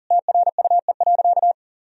Morse code, heard as a keyed tone, about 40 wpm, 700 Hz. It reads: TRUE.
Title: Sentences from Top 1000 Words - Review Part 4 - 40wpm